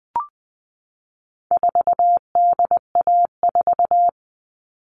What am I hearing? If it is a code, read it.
Morse code, heard as a keyed tone, about 20 wpm, 700 Hz. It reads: 4DA4
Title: All Cut Numbers - 4-Digit 20wpm